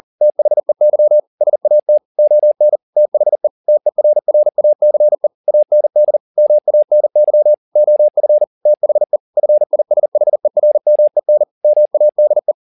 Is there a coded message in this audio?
Yes: Morse code from 0.2 to 12.5 s, heard as a keyed tone, about 40 wpm, 600 Hz. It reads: THEY SAT ON THE TERRACE AND MANY OF THE FISHERMEN MADE